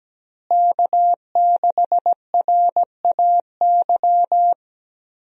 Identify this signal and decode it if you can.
Morse code, heard as a keyed tone, about 17 wpm, 700 Hz. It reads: K6RAY
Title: USA Call Signs - 5 Characters - Repeated 2x - 17wpm